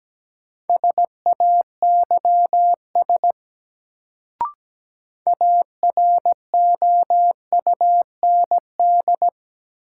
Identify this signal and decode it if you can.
Morse code, heard as a keyed tone, about 17 wpm, 700 Hz. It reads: SAYS AROUND
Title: Sets of 2 Words - New Word Drill - 17wpm